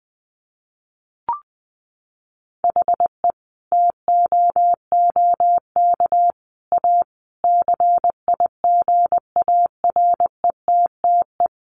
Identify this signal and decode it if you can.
Morse code, heard as a keyed tone, about 19 wpm, 700 Hz. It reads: HE TOOK A CIGARETTE
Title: Sentences from Top 1000 Words 20wpm